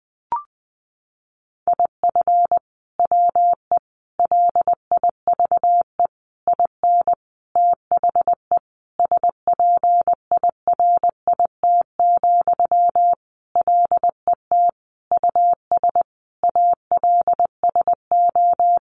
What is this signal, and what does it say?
Morse code, heard as a keyed tone, about 20 wpm, 700 Hz. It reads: IF WE LIVE IN THE SPIRIT, LET US ALSO
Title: Sentences from Top 1400 Words 20wpm